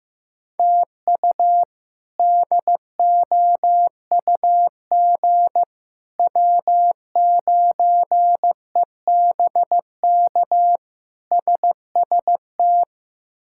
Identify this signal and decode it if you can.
Morse code, heard as a keyed tone, about 15 wpm, 700 Hz. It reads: TU DOUG W9EBK SST
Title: K1USN SST QSO Practice - 15wpm